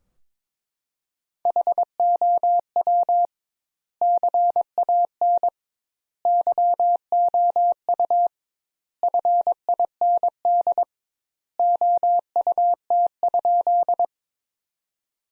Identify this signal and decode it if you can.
Morse code, heard as a keyed tone, about 22 wpm, 700 Hz. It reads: HOW CAN YOU FIND OUT?
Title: Sentences Top 200 Words - 22wpm